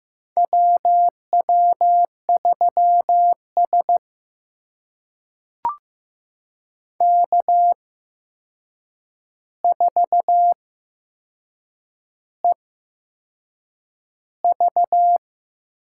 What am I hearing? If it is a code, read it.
Morse code, heard as a keyed tone, about 15 wpm, spaced as for 6 wpm, 700 Hz. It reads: WW3S K4EV
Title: USA Call Signs - 4 Characters - 8x Character Spacing - 15wpm